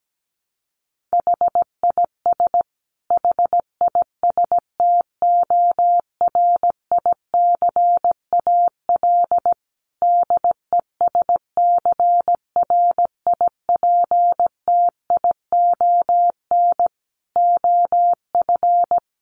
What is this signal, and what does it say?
Morse code, heard as a keyed tone, about 17 wpm, 700 Hz. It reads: HIS HISTORICAL DESCRIPTION OF
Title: Sentences from Top 1300 Words 17wpm